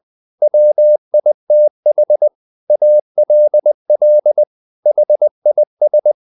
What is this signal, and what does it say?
Morse code, heard as a keyed tone, about 20 wpm, 600 Hz. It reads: WITH ALL HIS
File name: the old man and the sea part 2 20wpm